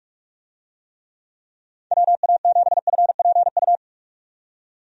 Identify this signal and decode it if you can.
Morse code, heard as a keyed tone, about 41 wpm, 700 Hz. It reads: WA7FPU
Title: USA Call Signs - 6 Characters - 45wpm